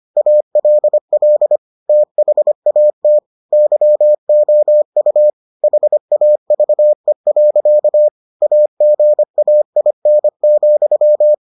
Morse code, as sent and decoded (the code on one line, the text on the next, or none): .- .-.. .-.. / - .... .- - / -.-- --- ..- / .... .- ...- . .-.-.- / .- --. .- .. -. --..--
ALL THAT YOU HAVE. AGAIN,